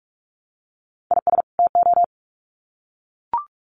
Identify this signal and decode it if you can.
Morse code, heard as a keyed tone, about 44 wpm, 700 Hz. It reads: IS TO
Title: Sets of 2 Words 45wpm